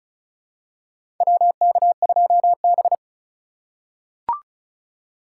Morse code, heard as a keyed tone, about 36 wpm, 700 Hz. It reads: WK2B
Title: USA Call Signs - 4 Characters - 4x Character Spacing - 35wpm